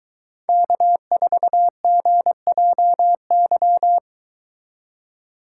Morse code, heard as a keyed tone, about 22 wpm, 700 Hz. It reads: K4GJY